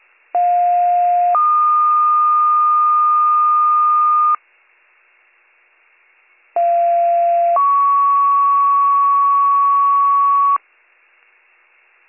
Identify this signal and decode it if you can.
Two-tone page, heard as a one-second tone, then a three-second tone, about 710 then 1220 Hz, about 690 then 1120 Hz. It is heard from 0.3 to 10.6 s.